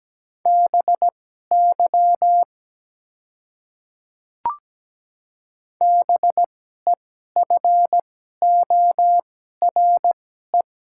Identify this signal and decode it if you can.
Morse code, heard as a keyed tone, about 17 wpm, spaced as for 12 wpm, 700 Hz. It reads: BY BEFORE
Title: 2x Character Spacing - Top 300 Words - 17wpm